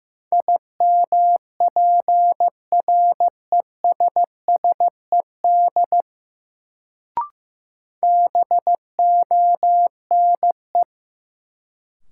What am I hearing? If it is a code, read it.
Morse code, heard as a keyed tone, about 15 wpm, 700 Hz. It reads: IMPRESSED BONE
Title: Sentences from Top 1600 Words - New Word Drill Alt - 15wpm